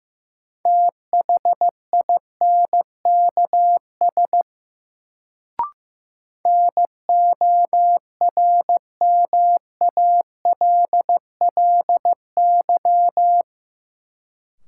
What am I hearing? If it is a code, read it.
Morse code, heard as a keyed tone, about 15 wpm, 700 Hz. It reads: THINKS NORMALLY
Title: Sentences from Top 1000 Words - Review Part 6 - New Word Drill - 15wpm